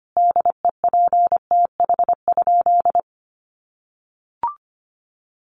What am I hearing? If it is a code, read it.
Morse code, heard as a keyed tone, about 25 wpm, 700 Hz. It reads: DEPTH?
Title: Sentences from Top 1200 Words 25wpm